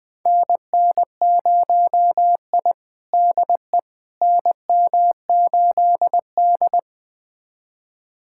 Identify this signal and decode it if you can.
Morse code, heard as a keyed tone, about 20 wpm, 700 Hz. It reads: NN0I DE NM8D
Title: USA Callsign DE Callsign - 4 Characters - 20wpm